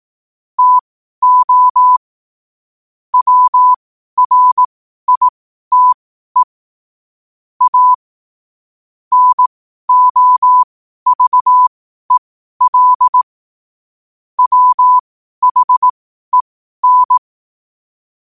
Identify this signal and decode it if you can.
Morse code, heard as a keyed tone, about 18 wpm, spaced as for 12 wpm, 1000 Hz. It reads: TO WRITE A NOVEL WHEN
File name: qotdfiles / 12wpm